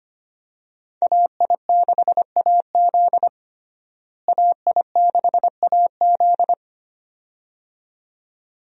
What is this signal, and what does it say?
Morse code, heard as a keyed tone, about 25 wpm, 700 Hz. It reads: AI6AZ AI6AZ
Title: USA Call Signs - 5 Characters - Repeated 2x - 25wpm